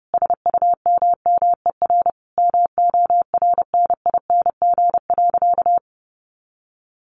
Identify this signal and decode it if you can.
Morse code, heard as a keyed tone, about 30 wpm, 700 Hz. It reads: SUMMER MORNING.
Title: Sentences from Top 1100 Words 30wpm